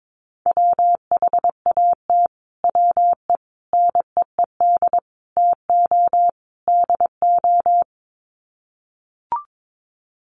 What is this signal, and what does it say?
Morse code, heard as a keyed tone, about 22 wpm, 700 Hz. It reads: WHAT WE NEED TO DO
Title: Sets of 5 Words Encore 22wpm